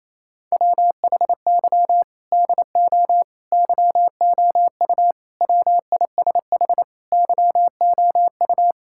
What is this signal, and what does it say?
Morse code, heard as a keyed tone, about 27 wpm, 700 Hz. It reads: WHY DO YOU WISH YOU
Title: Sentences from Top 1600 Words 28wpm